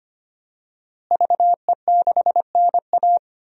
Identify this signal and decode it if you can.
Morse code, heard as a keyed tone, about 25 wpm, 700 Hz. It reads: VE6NA
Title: DXCC-100 Call Signs - 5 Characters - Repeated 2x - 25wpm